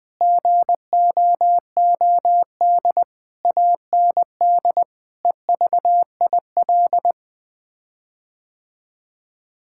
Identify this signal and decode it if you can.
Morse code, heard as a keyed tone, about 20 wpm, 700 Hz. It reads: GOOD AND EVIL
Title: Sets of 9 Words - 20wpm